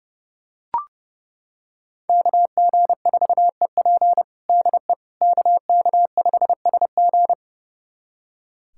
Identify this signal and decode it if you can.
Morse code, heard as a keyed tone, about 30 wpm, 700 Hz. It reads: KG4EP DE KK5SG
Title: USA Callsign DE Callsign - 5 Characters - 30wpm